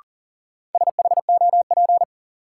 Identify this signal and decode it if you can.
Morse code, heard as a keyed tone, about 40 wpm, 700 Hz. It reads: SHOP
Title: Sentences from Top 1600 Words - New Word Drill - 40wpm